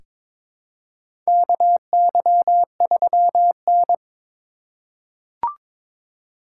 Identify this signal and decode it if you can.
Morse code, heard as a keyed tone, about 22 wpm, 700 Hz. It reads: KY3N
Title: USA Call Signs - 4 Characters - 2x Character Spacing - 22wpm